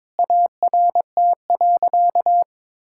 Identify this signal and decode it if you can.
Morse code, heard as a keyed tone, about 22 wpm, 700 Hz. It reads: ART.
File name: practice / 22wpm / Sentences with 4-Word Phrases - 22wpm